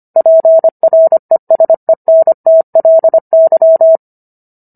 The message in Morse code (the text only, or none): PRESENTLY